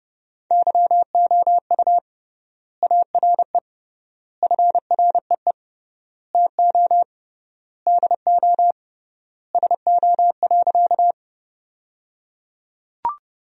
Morse code, heard as a keyed tone, about 30 wpm, 700 Hz. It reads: YOU ARE FREE TO DO SO.
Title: 3x Word-Spacing - Sentences from Top 500 Words - 30wpm